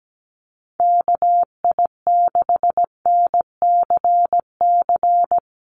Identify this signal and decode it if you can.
Morse code, heard as a keyed tone, about 17 wpm, 700 Hz. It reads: KI6NCC
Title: USA Call Signs Repeated 3x 17wpm